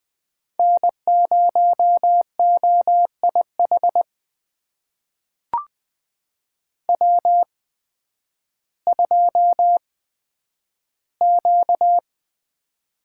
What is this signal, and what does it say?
Morse code, heard as a keyed tone, about 20 wpm, spaced as for 11 wpm, 700 Hz. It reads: N0OIH W2Q
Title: USA Call Signs - 5 Characters - 8x Character Spacing - 20wpm